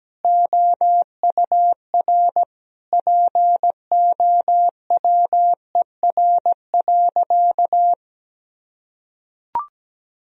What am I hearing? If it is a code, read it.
Morse code, heard as a keyed tone, about 17 wpm, 700 Hz. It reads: OUR POWER.
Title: Sentences from Top 1000 Words - Review Part 2 - 17wpm